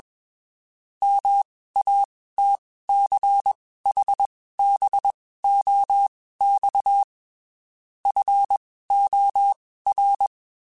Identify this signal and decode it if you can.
Morse code, heard as a keyed tone, about 21 wpm, spaced as for 15 wpm, 800 Hz. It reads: MATCHBOX FOR
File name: QSOs / QSOs-mp3s / WD0FBT 1@20